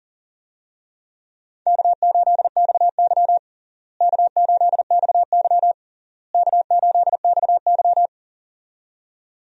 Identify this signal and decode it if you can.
Morse code, heard as a keyed tone, about 37 wpm, 700 Hz. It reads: K8XY K8XY K8XY
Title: USA Call Signs - 4 Characters - Repeated 3x - 40wpm